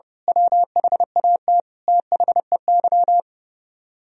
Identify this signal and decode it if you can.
Morse code, heard as a keyed tone, about 29 wpm, 700 Hz. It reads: WHAT THEY
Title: Speed-Racing - Sets of 5 Words 30wpm